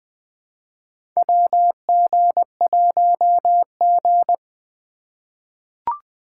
Morse code, heard as a keyed tone, about 21 wpm, 700 Hz. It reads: WG1G
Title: USA Call Signs - 2x Character Spacing - 20wpm